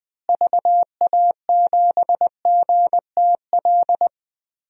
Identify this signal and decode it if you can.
Morse code, heard as a keyed tone, about 20 wpm, 700 Hz. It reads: VA7GTL
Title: DXCC-100 Call Signs - 6 Characters - Repeated 3x - 20wpm